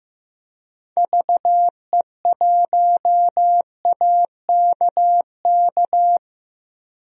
Morse code, heard as a keyed tone, about 15 wpm, 700 Hz. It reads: VE1AKK
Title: DXCC-100 Call Signs - 6 Characters - 15wpm